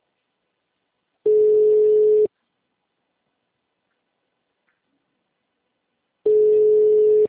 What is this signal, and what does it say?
Ringback tone (Europe, 425 Hz).